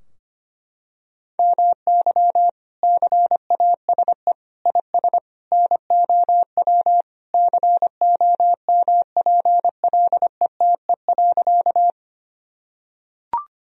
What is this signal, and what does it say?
Morse code, heard as a keyed tone, about 25 wpm, 700 Hz. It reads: MY CASE IS NOW COMPLETE.
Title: Sentences from Top 1000 Words - Review Part 2 - 25wpm